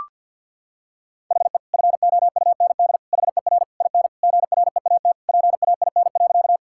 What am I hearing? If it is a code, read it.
Morse code, heard as a keyed tone, about 51 wpm, 700 Hz. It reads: HE FOUND HER IN GREAT PAIN.
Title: Sentences from Top 1600 Words 50wpm